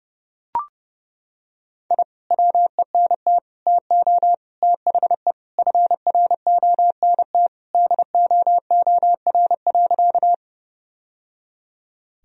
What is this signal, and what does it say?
Morse code, heard as a keyed tone, about 30 wpm, 700 Hz. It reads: I WENT TO THE FRONT DOOR.
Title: Sentences from Top 1000 Words - Review Part 2 - 30wpm